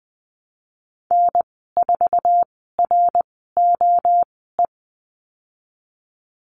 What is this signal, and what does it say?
Morse code, heard as a keyed tone, about 20 wpm, spaced as for 15 wpm, 700 Hz. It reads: N4ROE
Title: 173 Lesson 28g - Number 9 - Call Signs 20wpm